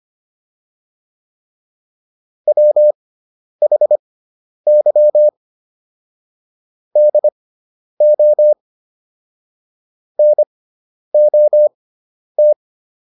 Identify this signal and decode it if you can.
Morse code, heard as a keyed tone, about 25 wpm, spaced as for 10 wpm, 600 Hz. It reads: WHY DO NOT